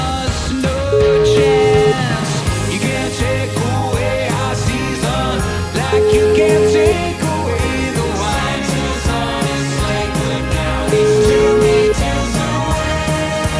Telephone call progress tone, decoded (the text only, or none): ringback tone